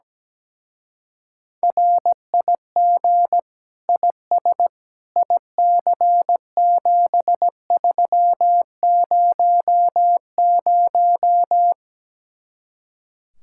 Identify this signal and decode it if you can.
Morse code, heard as a keyed tone, about 17 wpm, 700 Hz. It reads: RIG IS IC7300
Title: Speed-Racing Common QSO Phrases - 17wpm